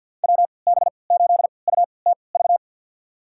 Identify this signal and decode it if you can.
Morse code, heard as a keyed tone, about 52 wpm, spaced as for 29 wpm, 700 Hz. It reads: WB8UTV
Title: USA Call Signs - 6 Characters - 3x Character Spacing - 50wpm